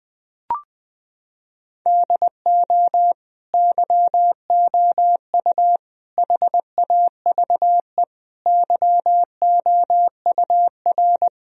Morse code, heard as a keyed tone, about 21 wpm, 700 Hz. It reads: DO YOU HAVE YOUR
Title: Sentences from Top 1000 Words - Review Part 4 - 20wpm